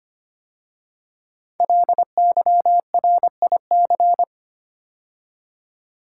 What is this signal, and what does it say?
Morse code, heard as a keyed tone, about 25 wpm, 700 Hz. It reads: LYRIC
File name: 5 Letter Words - First 3 Letters Given - 25wpm